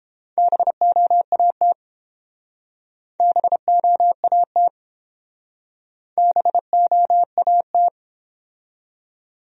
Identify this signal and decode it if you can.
Morse code, heard as a keyed tone, about 29 wpm, 700 Hz. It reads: BOAT BOAT BOAT